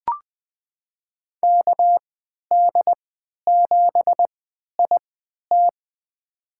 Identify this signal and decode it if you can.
Morse code, heard as a keyed tone, about 20 wpm, spaced as for 11 wpm, 700 Hz. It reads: KD7IT
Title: USA Call Signs - 3x Character Spacing - 20wpm